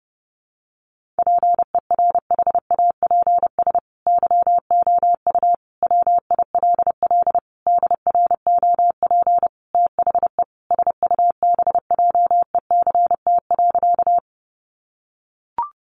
Morse code, heard as a keyed tone, about 30 wpm, 700 Hz. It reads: PERHAPS YOU WILL DROP THE SUBJECT.
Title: Sentences from Top 1400 Words 30wpm